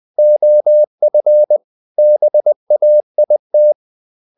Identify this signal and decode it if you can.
Morse code, heard as a keyed tone, about 20 wpm, 600 Hz. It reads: OF BAIT
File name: the old man and the sea / the old man and the sea part 1 20wpm